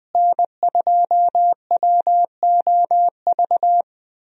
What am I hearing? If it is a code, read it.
Morse code, heard as a keyed tone, about 21 wpm, 700 Hz. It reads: N2WOV
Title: USA Call Signs - 5 Characters - Repeated 2x - 20wpm